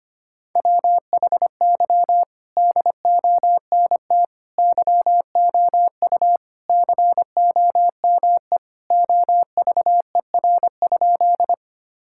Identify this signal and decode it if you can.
Morse code, heard as a keyed tone, about 25 wpm, 700 Hz. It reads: WHY DONT YOU COME OVER?
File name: practice / 25wpm / Sentences with 4-Word Phrases - 25wpm